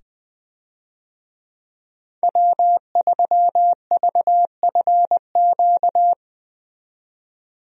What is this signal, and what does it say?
Morse code, heard as a keyed tone, about 20 wpm, 700 Hz. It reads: W3VFQ
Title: USA Call Signs - 5 Characters - 20wpm